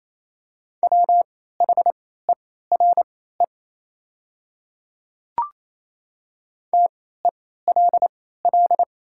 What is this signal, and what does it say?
Morse code, heard as a keyed tone, about 27 wpm, spaced as for 16 wpm, 700 Hz. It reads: WHERE TELL
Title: 3x Character Spacing - Top 200 Words - 28wpm